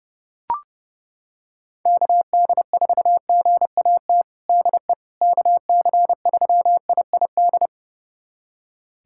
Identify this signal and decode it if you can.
Morse code, heard as a keyed tone, about 30 wpm, 700 Hz. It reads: KD4GAT DE KC3IID
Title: USA Callsign DE Callsign - 6 Characters - 30wpm